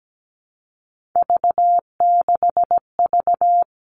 Morse code, heard as a keyed tone, about 17 wpm, 700 Hz. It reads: V6V
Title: All Cut Numbers - 3-Digit 17wpm